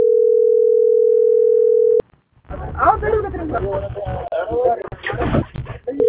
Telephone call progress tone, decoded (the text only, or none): ringback tone